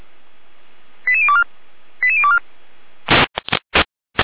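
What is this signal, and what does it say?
Five-tone selective call (ZVEI), heard as quick runs of five tones, tones of about 70 ms. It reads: 88024 88024
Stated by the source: ZVEI five-tone call 88024 received via SDR.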